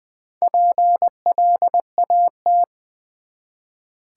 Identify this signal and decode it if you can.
Morse code, heard as a keyed tone, about 20 wpm, 700 Hz. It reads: PLAT
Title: Build A Word - 1 Letter At A Time - 3 to 5-Letter Words - 20wpm